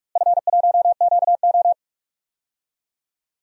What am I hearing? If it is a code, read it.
Morse code, heard as a keyed tone, about 42 wpm, 700 Hz. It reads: F1QO
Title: DXCC-100 Call Signs - 4 Characters - Repeated 2x - 45wpm